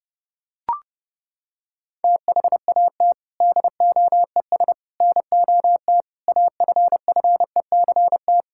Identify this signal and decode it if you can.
Morse code, heard as a keyed tone, about 30 wpm, 700 Hz. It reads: THAT DOES NOT AFFECT
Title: Sentences from Top 1000 Words - Review Part 6 - 30wpm